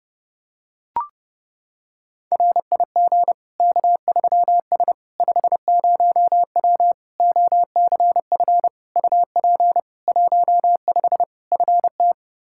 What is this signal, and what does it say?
Morse code, heard as a keyed tone, about 32 wpm, 700 Hz. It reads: RIG K3S 50W OCF UP 15 FT
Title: Practice QSO - Exchange 2 - Info - 30wpm